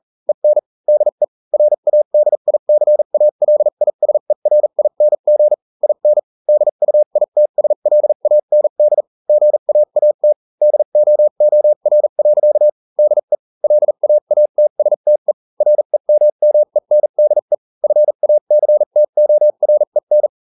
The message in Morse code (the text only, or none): EN DE RADICALISERING IN DUITSLAND GAAT DOOR. DE LAATSTE REMMENDE FACTOREN